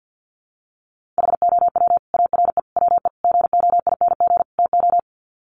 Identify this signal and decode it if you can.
Morse code, heard as a keyed tone, about 50 wpm, 700 Hz. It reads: HOW ARE WE GOING TO
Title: Sets of 5 Words Encore 50wpm